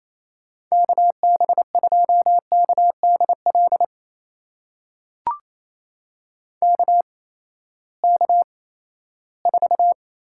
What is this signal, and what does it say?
Morse code, heard as a keyed tone, about 27 wpm, spaced as for 16 wpm, 700 Hz. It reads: KB2KDL KK4
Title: USA Call Signs - 6 Characters - 8x Character Spacing - 28wpm